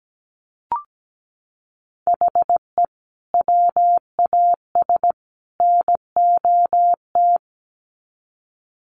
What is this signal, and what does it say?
Morse code, heard as a keyed tone, about 17 wpm, 700 Hz. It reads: HE WAS NOT
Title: Sets of 3 Words 17wpm